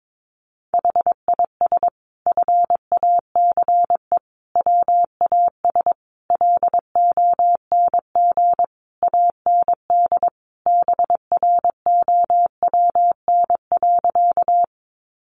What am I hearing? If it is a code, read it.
Morse code, heard as a keyed tone, about 22 wpm, 700 Hz. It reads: HIS FACE WAS LONG AND BROWN.
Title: Sentences from Top 1200 Words 22wpm